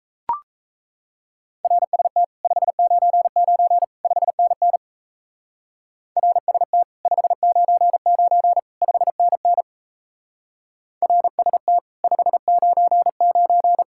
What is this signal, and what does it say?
Morse code, heard as a keyed tone, about 34 wpm, 700 Hz. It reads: RST 599 5NN RST 599 5NN RST 599
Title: Speed-Racing Common QSO Phrases - 28wpm